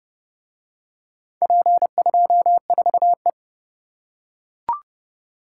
Morse code, heard as a keyed tone, about 27 wpm, 700 Hz. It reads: P24E